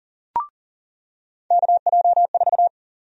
Morse code, heard as a keyed tone, about 40 wpm, 700 Hz. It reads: KJ4
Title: USA Call Sign Prefixes - 40wpm